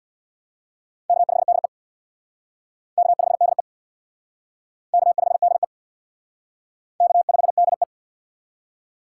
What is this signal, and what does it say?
Morse code, heard as a keyed tone, about 61 wpm, 700 Hz. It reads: K5DE K5DE K5DE K5DE
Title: Speed Racing - USA 1x2 Call Signs 50wpm